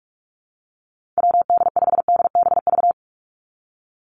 Morse code, heard as a keyed tone, about 44 wpm, 700 Hz. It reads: WD5DBV
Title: Call Signs Repeated 2x - 45wpm